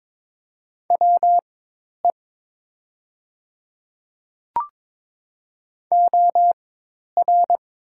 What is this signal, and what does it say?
Morse code, heard as a keyed tone, about 22 wpm, spaced as for 10 wpm, 700 Hz. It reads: WE OR
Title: 4x Character Spacing - Top 100 Words - 22wpm